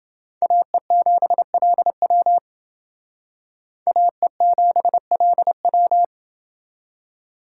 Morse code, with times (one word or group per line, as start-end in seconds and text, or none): 0.4-2.4 AE7LW
3.9-6.0 AE7LW